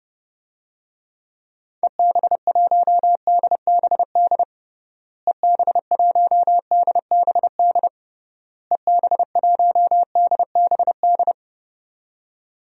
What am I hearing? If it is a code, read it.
Morse code, heard as a keyed tone, about 30 wpm, 700 Hz. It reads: EB1DBD EB1DBD EB1DBD
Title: DXCC-100 Call Signs - 6 Characters - Repeated 3x - 30wpm